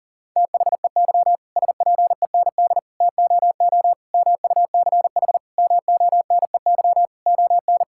Morse code, heard as a keyed tone, about 40 wpm, 700 Hz. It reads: THEY SPEND TOO MUCH MONEY ON
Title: Sentences from Top 1000 Words - Review Part 5 - 40wpm